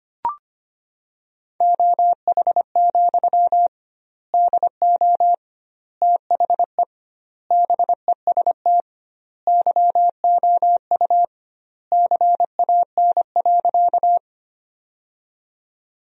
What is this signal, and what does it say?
Morse code, heard as a keyed tone, about 25 wpm, 700 Hz. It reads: OH, DO THE BEST YOU CAN.
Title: 2x Word-Spacing - Sentences from Top 500 Words - 25wpm